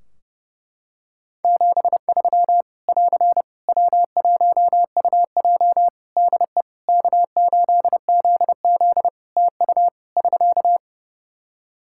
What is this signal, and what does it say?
Morse code, heard as a keyed tone, about 30 wpm, 700 Hz. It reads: 73 <AR> W1UJ DE K8ZZ TU <SK>
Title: Practice QSO - Exchange 4 - Ending - 30wpm